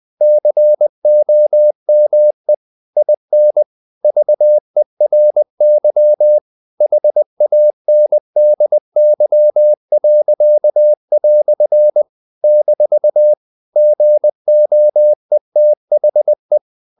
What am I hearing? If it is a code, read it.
Morse code, heard as a keyed tone, about 20 wpm, 600 Hz. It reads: COME IN VERY HANDY." - GOETHE
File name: CW / Practice / EnglishQuotations0320